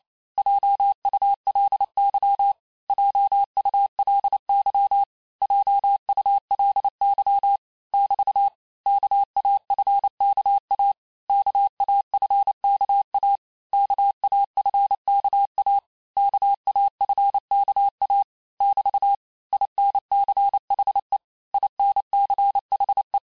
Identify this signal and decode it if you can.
Morse code, heard as a keyed tone, about 28 wpm, 800 Hz. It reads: JULY JULY JULY <BT> KAFKA KAFKA KAFKA KAFKA <BT> INCHE INCHE
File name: cours-du-mercredi / audios / cours-865-teste-les-mots-vitesse-27mpm